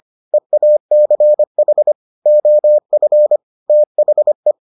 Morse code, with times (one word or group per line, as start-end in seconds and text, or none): 0.3-1.9 EACH
2.3-3.4 OF
3.7-4.5 THE